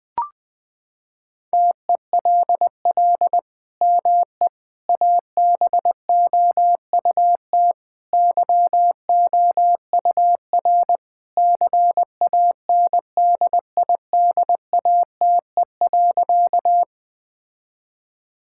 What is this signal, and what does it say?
Morse code, heard as a keyed tone, about 20 wpm, 700 Hz. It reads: TELL ME ABOUT YOUR CANDIDATE.
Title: Sentences from Top 1600 Words 20wpm